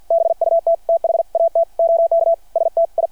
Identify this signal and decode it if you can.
Morse code, heard as a keyed tone, about 53 wpm, 690 Hz. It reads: BUT THAT OK STI
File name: archived / CRSnet